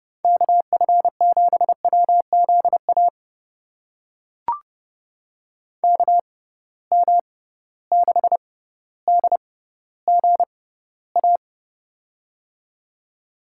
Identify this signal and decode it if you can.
Morse code, heard as a keyed tone, about 30 wpm, spaced as for 15 wpm, 700 Hz. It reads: KF7WZA KM6DGA
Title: USA Call Signs - 6 Characters - 6x Character Spacing - 30wpm